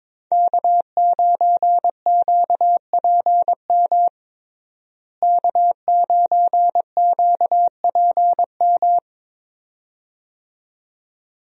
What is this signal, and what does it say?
Morse code, heard as a keyed tone, about 22 wpm, 700 Hz. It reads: K9QPM K9QPM
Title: USA Call Signs - 5 Characters - Repeated 2x - 22wpm